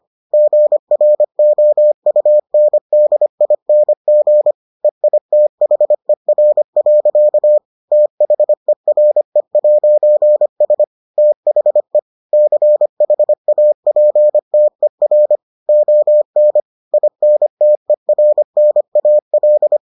Morse code, heard as a keyed tone, about 25 wpm, 600 Hz. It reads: GROUNDING EITHER. THERE'S THE CHAPTER ON INTERNAL